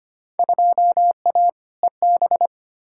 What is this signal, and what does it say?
Morse code, heard as a keyed tone, about 25 wpm, 700 Hz. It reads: 2A EB
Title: Field Day Exchanges - 25wpm